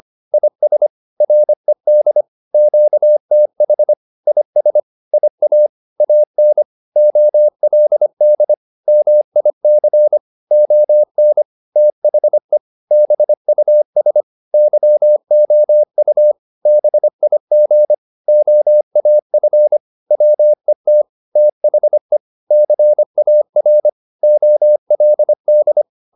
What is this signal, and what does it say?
Morse code, heard as a keyed tone, about 25 wpm, 600 Hz. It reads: IS RED QTH IS IA AN OLD MIC ON THE BUS YOU BIG OAF WET THE CAR OLD
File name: PR204a 25